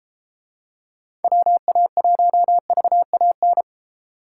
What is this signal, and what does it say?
Morse code, heard as a keyed tone, about 30 wpm, 700 Hz. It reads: WA1VAN